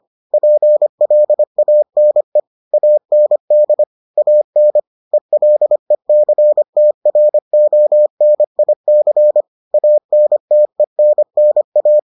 Morse code, heard as a keyed tone, about 25 wpm, 600 Hz. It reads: PLANE AND AN ELECTRONIC ANTENNA